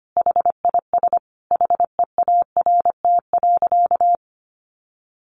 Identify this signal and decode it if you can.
Morse code, heard as a keyed tone, about 25 wpm, 700 Hz. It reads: HIS HEART.
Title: Sentences from Top 1200 Words 25wpm